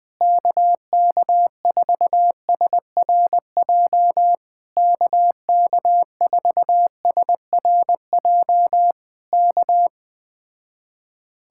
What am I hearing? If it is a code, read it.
Morse code, heard as a keyed tone, about 20 wpm, 700 Hz. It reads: KK4SRJ KK4SRJ K
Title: Speed-Racing Common QSO Phrases - 17wpm